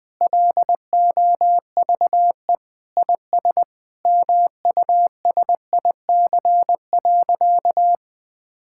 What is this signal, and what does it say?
Morse code, heard as a keyed tone, about 20 wpm, 700 Hz. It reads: LOVE IS MUSIC.
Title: Sentences with 5-Word Phrases - 20wpm